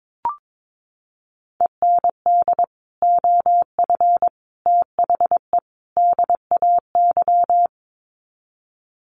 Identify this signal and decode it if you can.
Morse code, heard as a keyed tone, about 22 wpm, 700 Hz. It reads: END OF THE DAY
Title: Sets of 4 Words 22wpm